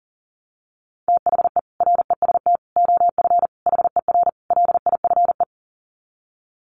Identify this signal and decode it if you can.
Morse code, heard as a keyed tone, about 40 wpm, 700 Hz. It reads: THE REST OF HER LIFE
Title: Sets of 5 Words Encore 40wpm